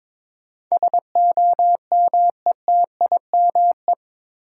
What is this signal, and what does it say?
Morse code, heard as a keyed tone, about 21 wpm, 700 Hz. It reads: SOMETIME
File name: practice / 22wpm / Eight Letter Words - 22wpm